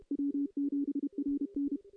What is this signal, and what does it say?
Morse code, heard as a keyed tone, about 32 wpm, 280 Hz. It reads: W7RN
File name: contests / CW-SS-k5tr- -1500